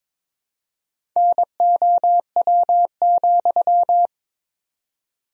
Morse code, heard as a keyed tone, about 22 wpm, 700 Hz. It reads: NOW,
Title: 4x Word-Spacing - Sentences from Top 200 Words - 22wpm